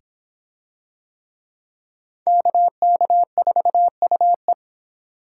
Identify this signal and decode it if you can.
Morse code, heard as a keyed tone, about 23 wpm, 700 Hz. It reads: KK4UE